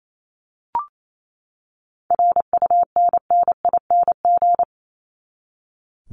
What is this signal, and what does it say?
Morse code, heard as a keyed tone, about 28 wpm, 700 Hz. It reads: RUNNING
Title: ING Suffix - Top 100 Words with Base Words 28wpm